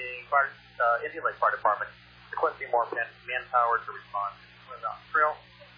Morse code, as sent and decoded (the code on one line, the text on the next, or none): none